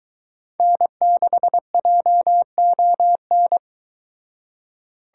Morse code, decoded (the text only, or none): N6JON